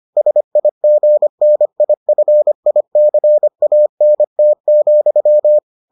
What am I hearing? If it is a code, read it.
Morse code, heard as a keyed tone, about 25 wpm, 600 Hz. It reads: SIGNIFICANT,